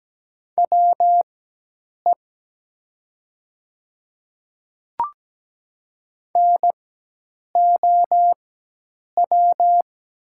Morse code, heard as a keyed tone, about 17 wpm, spaced as for 8 wpm, 700 Hz. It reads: WE NOW
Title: 4x Character Spacing - Top 100 Words - 17wpm